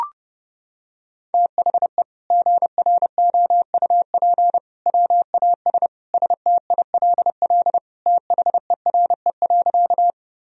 Morse code, heard as a keyed tone, about 30 wpm, 700 Hz. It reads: THE GROUP WAS STILL THERE.
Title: Sentences from Top 1000 Words - Review Part 1 - 30wpm